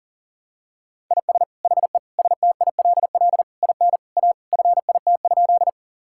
Morse code, heard as a keyed tone, about 40 wpm, 700 Hz. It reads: IS HE STILL IN A FIT?
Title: Sentences from Top 1000 Words - Review Part 5 - 40wpm